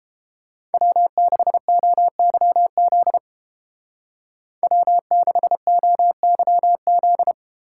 Morse code, heard as a keyed tone, about 30 wpm, 700 Hz. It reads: W6OYZ W6OYZ